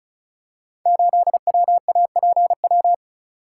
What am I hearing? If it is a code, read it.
Morse code, heard as a keyed tone, about 36 wpm, 700 Hz. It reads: 8WAPW
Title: Code Groups - 5 Letter:Numbers - 35wpm